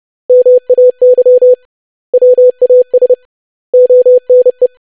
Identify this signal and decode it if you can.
Morse code, heard as a keyed tone, about 30 wpm, 500 Hz. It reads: MAY WAS ONE
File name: lesson / web / basic100-25wpm